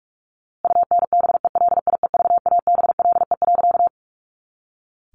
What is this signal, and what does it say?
Morse code, heard as a keyed tone, about 44 wpm, 700 Hz. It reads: UNBELIEVABLE.
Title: A Sound Of Thunder 45wpm